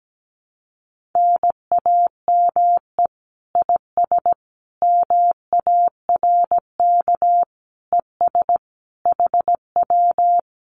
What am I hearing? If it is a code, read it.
Morse code, heard as a keyed tone, about 17 wpm, 700 Hz. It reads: NAME IS MARK ES HW
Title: Common QSO Phrases Encore 17wpm